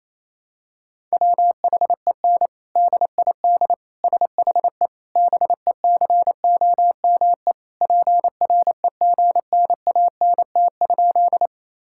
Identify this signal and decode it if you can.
Morse code, heard as a keyed tone, about 27 wpm, 700 Hz. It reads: WHEN DID SHE BECOME PREGNANT?
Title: Sentences from Top 1000 Words - Review Part 6 - 28wpm